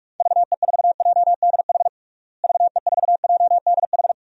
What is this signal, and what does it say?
Morse code, heard as a keyed tone, about 43 wpm, 700 Hz. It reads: VE4JDH VE4JDH
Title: DXCC-100 Call Signs - 6 Characters - Repeated 2x - 45wpm